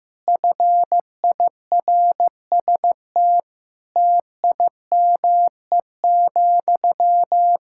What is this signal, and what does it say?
Morse code, heard as a keyed tone, about 16 wpm, 700 Hz. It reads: FIRST TIME,
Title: Sentences from Top 1000 Words - Review Part 6 - 15wpm